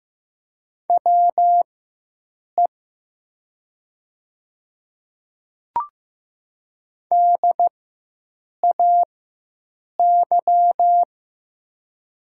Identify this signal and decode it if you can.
Morse code, heard as a keyed tone, about 15 wpm, spaced as for 7 wpm, 700 Hz. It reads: WE DAY
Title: 4x Character Spacing - Top 100 Words - 15wpm